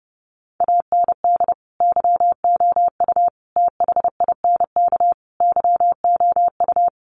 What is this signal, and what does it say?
Morse code, heard as a keyed tone, about 30 wpm, 700 Hz. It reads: AND YOU THINK YOU
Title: Sentences from Top 1300 Words 30wpm